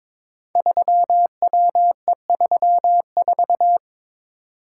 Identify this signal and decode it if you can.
Morse code, heard as a keyed tone, about 23 wpm, 700 Hz. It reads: 3WE34
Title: Code Groups - 5 Letter:Numbers - 22wpm